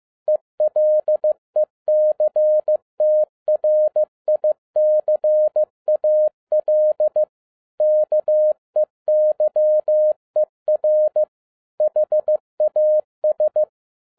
Morse code, heard as a keyed tone, about 15 wpm, 610 Hz. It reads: ELECTRICAL KEYER HAS